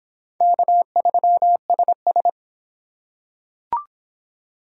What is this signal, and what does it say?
Morse code, heard as a keyed tone, about 25 wpm, 700 Hz. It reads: K3SS